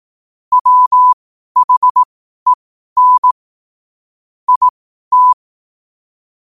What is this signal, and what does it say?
Morse code, heard as a keyed tone, about 18 wpm, spaced as for 12 wpm, 1000 Hz. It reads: WHEN IT
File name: qotdfiles / 12wpm